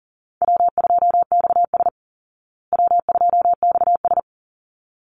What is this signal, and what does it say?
Morse code, heard as a keyed tone, about 40 wpm, 700 Hz. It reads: W2XS W2XS
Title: Call Signs Repeated 3x - 40wpm